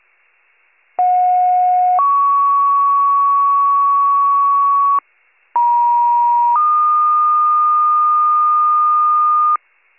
Two-tone page, heard as a one-second tone, then a three-second tone, about 730 then 1120 Hz, about 950 then 1220 Hz. It is heard from 1.0 to 9.6 s.